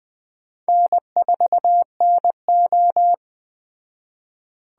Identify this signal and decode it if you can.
Morse code, heard as a keyed tone, about 20 wpm, 700 Hz. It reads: N4NO
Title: Speed Racing - USA 1x2 Call Signs 17wpm